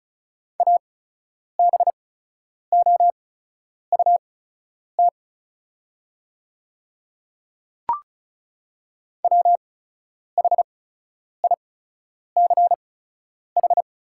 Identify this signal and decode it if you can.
Morse code, heard as a keyed tone, about 34 wpm, spaced as for 10 wpm, 700 Hz. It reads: ABOUT WHICH
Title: 8x Character Spacing - Top 100 Words - 35wpm